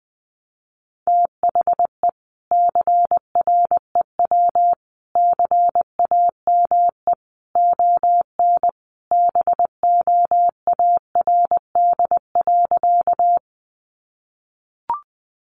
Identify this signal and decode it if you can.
Morse code, heard as a keyed tone, about 20 wpm, 700 Hz. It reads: THE CREW CAME ON BOARD.
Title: Sentences from Top 1500 Words 20wpm